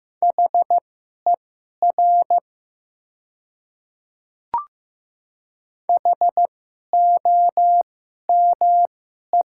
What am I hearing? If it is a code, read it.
Morse code, heard as a keyed tone, about 15 wpm, spaced as for 11 wpm, 700 Hz. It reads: HER HOME
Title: 2x Character Spacing - Top 200 Words - 15wpm